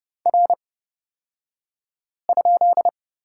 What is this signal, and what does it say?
Morse code, heard as a keyed tone, about 30 wpm, spaced as for 5 wpm, 700 Hz. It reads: R?
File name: Lesson 37d - Letter x - Two Characters 30wpm